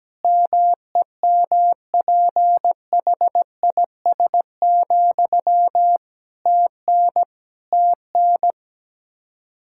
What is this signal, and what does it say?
Morse code, heard as a keyed tone, about 17 wpm, 700 Hz. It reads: MEMPHIS, TN TN
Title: Practice QSO - Exchange 1 - Beginning - 17wpm